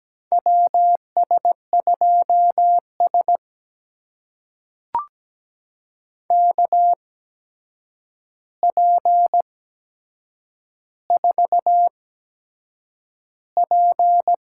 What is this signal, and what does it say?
Morse code, heard as a keyed tone, about 17 wpm, spaced as for 7 wpm, 700 Hz. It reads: WS2S KP4P
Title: USA Call Signs - 4 Characters - 8x Character Spacing - 17wpm